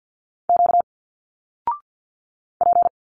Morse code, heard as a keyed tone, about 50 wpm, 700 Hz. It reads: Q P
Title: Single Letters 50wpm